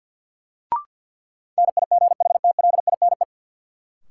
Morse code, heard as a keyed tone, about 53 wpm, 700 Hz. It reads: NIGHTLINE
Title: Sets of 5 Words - New Word Drill - 50wpm